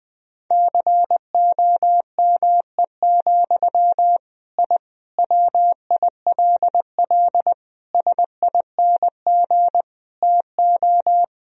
Morse code, heard as a keyed tone, about 20 wpm, 700 Hz. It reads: COME, I WILL SING TO
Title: Sentences from Top 1000 Words - Review Part 2 - 20wpm